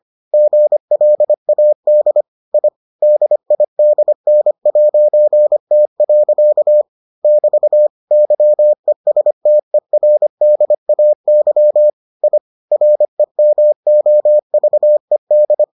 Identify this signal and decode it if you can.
Morse code, heard as a keyed tone, about 25 wpm, 600 Hz. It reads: GLAD I DIDN'T. <BT> YESTERDAY I REMOVED